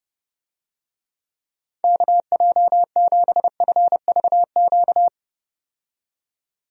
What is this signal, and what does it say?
Morse code, heard as a keyed tone, about 30 wpm, 700 Hz. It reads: KJ7FVQ
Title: USA Call Signs - 6 Characters - 30wpm